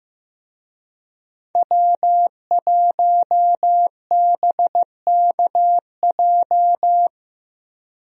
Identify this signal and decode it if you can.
Morse code, heard as a keyed tone, about 15 wpm, 700 Hz. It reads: W1BKJ
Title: USA Call Signs - 5 Characters - Repeated 2x - 15wpm